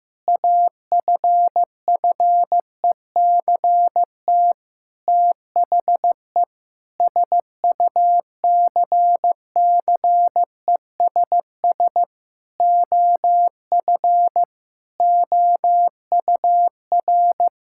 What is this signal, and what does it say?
Morse code, heard as a keyed tone, about 16 wpm, 700 Hz. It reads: AFFECT THE SUCCESS OF OUR
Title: Sentences from Top 1000 Words - Review Part 6 - 15wpm